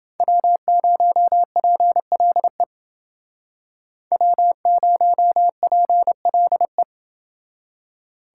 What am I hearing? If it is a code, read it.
Morse code, heard as a keyed tone, about 29 wpm, 700 Hz. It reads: W0PLE W0PLE